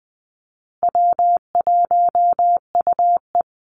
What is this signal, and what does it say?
Morse code, heard as a keyed tone, about 20 wpm, 700 Hz. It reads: W1UE
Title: Call Signs Repeated 3x - 20wpm